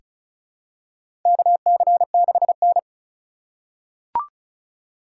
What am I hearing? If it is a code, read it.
Morse code, heard as a keyed tone, about 36 wpm, 700 Hz. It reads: KC6N
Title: USA Call Signs - 2x Character Spacing - 35wpm